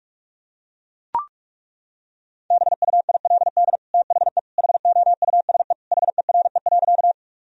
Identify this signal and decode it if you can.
Morse code, heard as a keyed tone, about 41 wpm, 700 Hz. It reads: BUILD THE HOUSE HERE.
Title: Sentences from Top 1000 Words - Review Part 1 - 45wpm